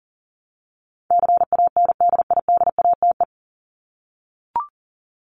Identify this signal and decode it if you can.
Morse code, heard as a keyed tone, about 40 wpm, 700 Hz. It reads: CANDIDATE
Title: Top 1501-1750 Words 40wpm